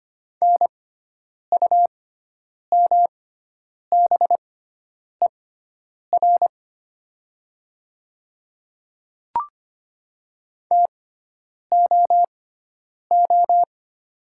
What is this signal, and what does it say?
Morse code, heard as a keyed tone, about 25 wpm, spaced as for 9 wpm, 700 Hz. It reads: NUMBER TOO
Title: 6x Character Spacing - Top 200 Words - 25wpm